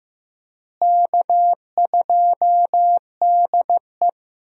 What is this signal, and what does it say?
Morse code, heard as a keyed tone, about 15 wpm, 700 Hz. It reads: K2DE
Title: USA Call Signs - 4 Characters - Repeated 2x - 15wpm